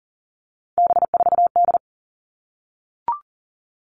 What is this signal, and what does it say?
Morse code, heard as a keyed tone, about 40 wpm, 700 Hz. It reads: B4D
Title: All Cut Numbers - 3-Digit 40wpm